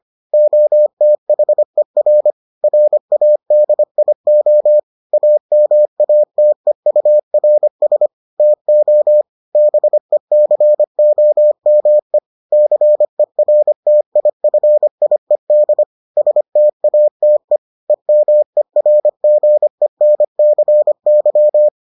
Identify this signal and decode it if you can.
Morse code, heard as a keyed tone, about 25 wpm, 600 Hz. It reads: OTHER RADIO AMATEURS TO BECOME CERTIFIED STATE EMERGENCY